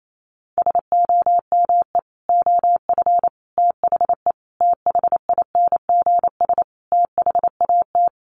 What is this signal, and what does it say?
Morse code, heard as a keyed tone, about 28 wpm, 700 Hz. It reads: SOME OF THE THINGS THAT
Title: Sets of 5 Words 28wpm